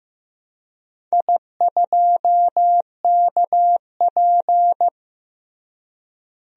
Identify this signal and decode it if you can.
Morse code, heard as a keyed tone, about 15 wpm, 700 Hz. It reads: I2KP
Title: DXCC-100 Call Signs - 4 Characters - Repeated 3x - 15wpm